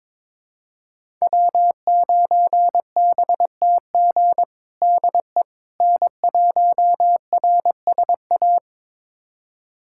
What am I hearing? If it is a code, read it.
Morse code, heard as a keyed tone, about 21 wpm, 700 Hz. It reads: W9BTG DE N1RSA
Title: USA Callsign DE Callsign - 5 Characters - 22wpm